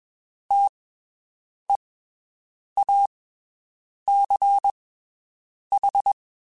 Morse code, heard as a keyed tone, about 21 wpm, spaced as for 7 wpm, 800 Hz. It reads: TEACH
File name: (spaced) / QSOs Spaced-mp3s / WD0BGP 1 @20 sp